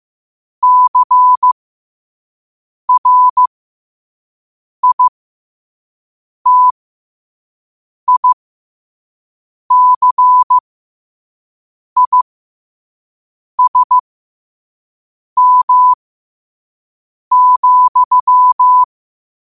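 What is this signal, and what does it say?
Morse code, heard as a keyed tone, about 15 wpm, spaced as for 5 wpm, 1000 Hz. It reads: CRITICISM,
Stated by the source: Morse code practice at 5 wpm.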